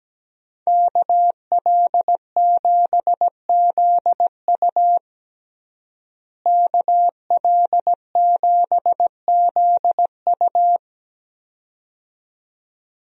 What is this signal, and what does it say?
Morse code, heard as a keyed tone, about 17 wpm, 700 Hz. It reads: KL7ZU KL7ZU
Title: USA Call Signs - 5 Characters - Repeated 2x - 17wpm